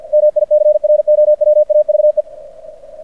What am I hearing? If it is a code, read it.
Morse code, heard as a keyed tone, about 50 wpm, 600 Hz. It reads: MICROWAVE
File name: archived / CRSnet